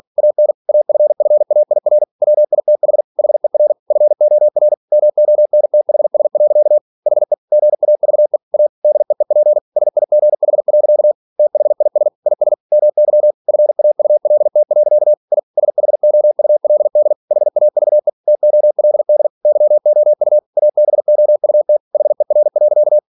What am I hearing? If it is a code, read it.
Morse code, heard as a keyed tone, about 46 wpm, 600 Hz. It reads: AN AFFAIR WITH HER FOR MONTHS. HE GAVE A DEEP SIGH. THIS IS MY FAULT. I SHOULD HAVE TOLD YOU ABOUT HER.
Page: Never 47wpm